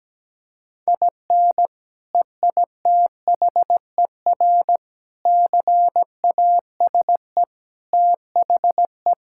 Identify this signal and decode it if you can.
Morse code, heard as a keyed tone, about 17 wpm, 700 Hz. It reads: IN EITHER CASE THE
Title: Sentences from Top 1000 Words - Review Part 6 - 17wpm